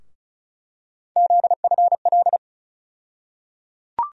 Morse code, heard as a keyed tone, about 36 wpm, 700 Hz. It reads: ZFL
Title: USA Call Sign Postfixes - 3 Letters 35wpm